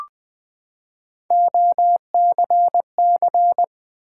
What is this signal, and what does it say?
Morse code, heard as a keyed tone, about 20 wpm, 700 Hz. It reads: OCC
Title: Build A Word - 3 Letters At A Time - 6-Letter Words - 20wpm